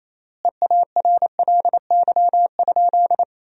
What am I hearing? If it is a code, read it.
Morse code, heard as a keyed tone, about 28 wpm, 700 Hz. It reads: EARLY?
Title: 3x Word-Spacing - Sentences from Top 400 Words - 28wpm